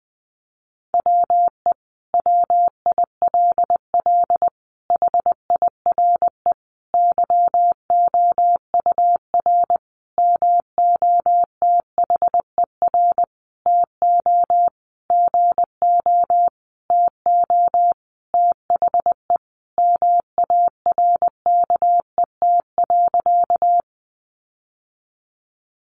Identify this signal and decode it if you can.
Morse code, heard as a keyed tone, about 20 wpm, 700 Hz. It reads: WE WILL HIRE YOUR MOTHER TO GO TO THE MARKET.
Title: Sentences from Top 1400 Words 20wpm